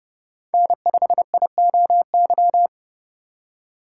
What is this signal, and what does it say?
Morse code, heard as a keyed tone, about 30 wpm, 700 Hz. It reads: N5IOY
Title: USA Call Signs - 5 Characters - 3x Character Spacing - 30wpm